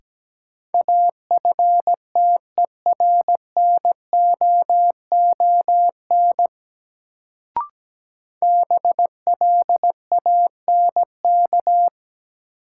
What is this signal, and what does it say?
Morse code, heard as a keyed tone, about 17 wpm, 700 Hz. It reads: AFTERNOON BLANK
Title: Sentences from Top 1500 Words - New Word Drill Alt - 17wpm